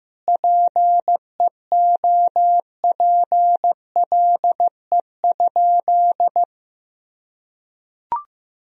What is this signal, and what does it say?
Morse code, heard as a keyed tone, about 15 wpm, 700 Hz. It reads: PEOPLE?
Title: Sentences from Top 1000 Words - Review Part 4 - 15wpm